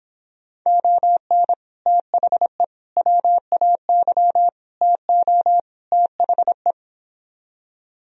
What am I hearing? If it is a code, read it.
Morse code, heard as a keyed tone, about 29 wpm, 700 Hz. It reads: ON THE WAY TO THE